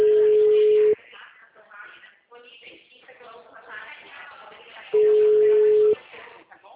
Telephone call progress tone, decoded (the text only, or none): ringback tone